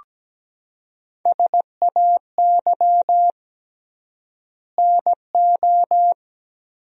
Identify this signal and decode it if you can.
Morse code, heard as a keyed tone, about 17 wpm, 700 Hz. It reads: SAY NO
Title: 3x Word-Spacing - Sentences from Top 200 Words - 17wpm